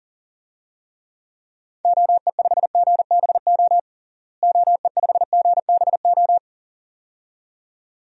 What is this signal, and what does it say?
Morse code, heard as a keyed tone, about 40 wpm, 700 Hz. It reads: OE5GBO OE5GBO
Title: DXCC-100 Call Signs - 6 Characters - Repeated 2x - 40wpm